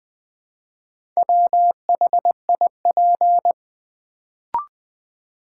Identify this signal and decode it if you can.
Morse code, heard as a keyed tone, about 21 wpm, 700 Hz. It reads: WHIP
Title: Build A Word - 2 Letters At A Time - 4-Letter Words - 20wpm